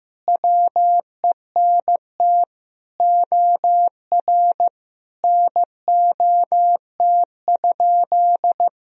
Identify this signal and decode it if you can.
Morse code, heard as a keyed tone, about 15 wpm, 700 Hz. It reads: WENT OR NOT?
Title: Sentences from Top 1000 Words - Review Part 3 - 15wpm